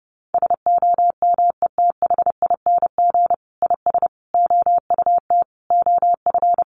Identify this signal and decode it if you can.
Morse code, heard as a keyed tone, about 30 wpm, 700 Hz. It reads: SOMETHING IS OUT OF
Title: Sentences from Top 1400 Words 30wpm